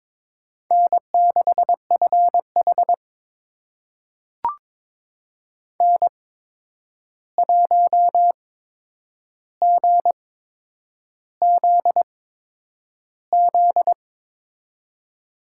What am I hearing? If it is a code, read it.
Morse code, heard as a keyed tone, about 22 wpm, spaced as for 9 wpm, 700 Hz. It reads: N6FH N1GZZ
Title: USA Call Signs - 8x Character Spacing - 22wpm